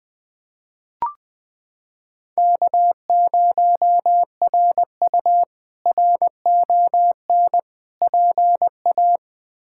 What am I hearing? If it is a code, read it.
Morse code, heard as a keyed tone, about 20 wpm, 700 Hz. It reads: K0RU RON PA
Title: K1USN SST QSO Practice - 20wpm